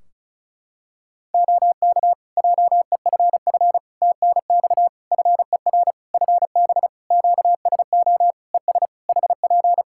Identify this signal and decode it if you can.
Morse code, heard as a keyed tone, about 36 wpm, 700 Hz. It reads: OK JEFF TNX FER FB QSO ES HP
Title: K1USN SST QSO Practice 35wpm